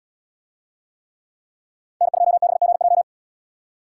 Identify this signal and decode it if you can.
Morse code, heard as a keyed tone, about 77 wpm, 700 Hz. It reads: N2BGJ